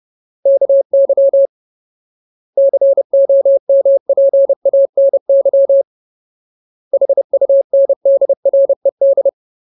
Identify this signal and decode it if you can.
Morse code, heard as a keyed tone, about 30 wpm, 550 Hz. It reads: KY COMPANY HUNDRED
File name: Ultimate Word List / 30WPM Ultimate